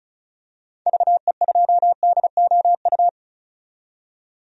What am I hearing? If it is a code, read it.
Morse code, heard as a keyed tone, about 34 wpm, 700 Hz. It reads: VE2DOU
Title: DXCC-100 Call Signs - 6 Characters - 35wpm